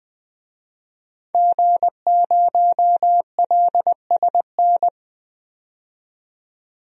Morse code, heard as a keyed tone, about 20 wpm, 700 Hz. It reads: G0LSN
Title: DXCC-100 Call Signs - 5 Characters - 20wpm